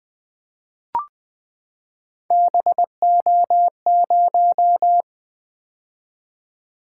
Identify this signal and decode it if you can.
Morse code, heard as a keyed tone, about 20 wpm, 700 Hz. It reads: BO0
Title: Three Letter-Number Combinations - 20wpm